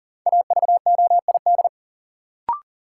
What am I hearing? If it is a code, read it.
Morse code, heard as a keyed tone, about 37 wpm, 700 Hz. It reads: AVOID
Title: Sentences from Top 1200 Words - New Word Drill Alt - 40wpm